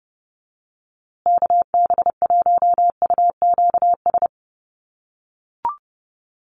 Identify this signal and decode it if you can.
Morse code, heard as a keyed tone, about 29 wpm, 700 Hz. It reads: KB1UQS
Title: USA Call Signs Repeated 3x 30wpm